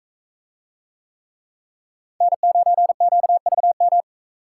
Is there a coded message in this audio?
Yes: Morse code from 2.2 to 4.0 s, heard as a keyed tone, about 42 wpm, 700 Hz. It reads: N9QVM